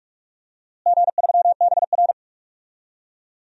Morse code, heard as a keyed tone, about 46 wpm, 700 Hz. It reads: G3BR
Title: DXCC-100 Call Signs - 4 Characters - Repeated 3x - 45wpm